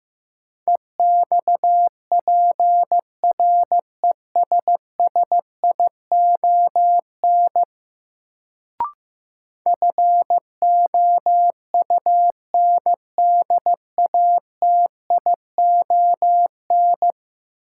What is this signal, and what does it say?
Morse code, heard as a keyed tone, about 15 wpm, 700 Hz. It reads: EXPRESSION FOUNDATION
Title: Sentences from Top 1200 Words - New Word Drill Alt - 15wpm